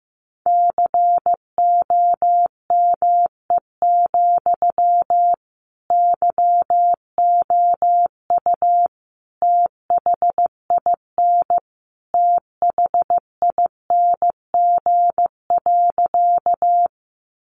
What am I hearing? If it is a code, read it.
Morse code, heard as a keyed tone, about 15 wpm, 700 Hz. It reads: COME, YOU THIN THING.
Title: Sentences from Top 1200 Words 15wpm